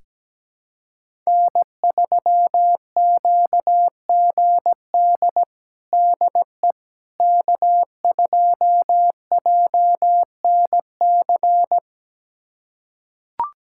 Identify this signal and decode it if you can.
Morse code, heard as a keyed tone, about 17 wpm, 700 Hz. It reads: N3QGD DE K2JNC
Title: USA Callsign DE Callsign - 5 Characters - 17wpm